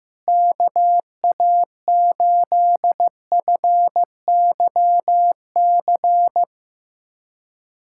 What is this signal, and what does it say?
Morse code, heard as a keyed tone, about 16 wpm, 700 Hz. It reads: KA8FYC
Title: USA Callsign DE Callsign - 6 Characters - 15wpm